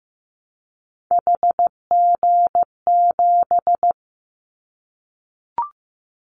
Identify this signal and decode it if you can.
Morse code, heard as a keyed tone, about 15 wpm, 700 Hz. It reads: HG7
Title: Callsign Prefixes for Top 25 DXCC - 15wpm